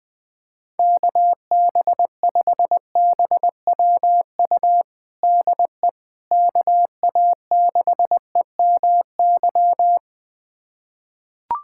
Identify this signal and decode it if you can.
Morse code, heard as a keyed tone, about 20 wpm, 700 Hz. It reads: KB5BWU DE KA6EMY
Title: USA Callsign DE Callsign - 6 Characters - 20wpm